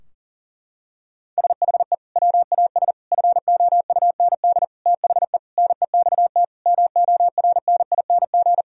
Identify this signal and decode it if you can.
Morse code, heard as a keyed tone, about 40 wpm, 700 Hz. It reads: SHE WAS FOUND THE NEXT MORNING
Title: Sentences from Top 1000 Words - Review Part 3 - 40wpm